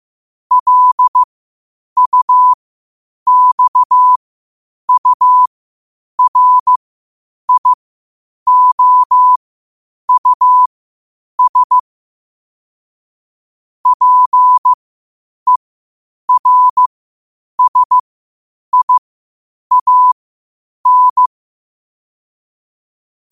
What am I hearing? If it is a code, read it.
Morse code, heard as a keyed tone, about 15 wpm, spaced as for 8 wpm, 1000 Hz. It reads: LUXURIOUS PERSIAN
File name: qotdfiles / 8wpm